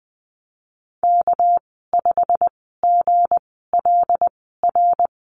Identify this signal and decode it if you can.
Morse code, heard as a keyed tone, about 20 wpm, spaced as for 14 wpm, 700 Hz. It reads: K5GLR
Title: 213 Lesson 34g - Number 8 - Call Signs 20wpm